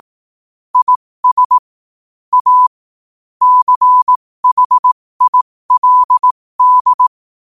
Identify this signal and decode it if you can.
Morse code, heard as a keyed tone, about 18 wpm, spaced as for 15 wpm, 1000 Hz. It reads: IS A CHILD
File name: qotdfiles / 15wpm